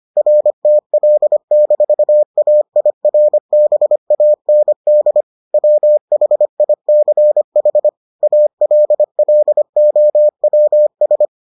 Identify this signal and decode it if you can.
Morse code, heard as a keyed tone, about 25 wpm, 600 Hz. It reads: RTL-AIRBAND WHICH ALLOWS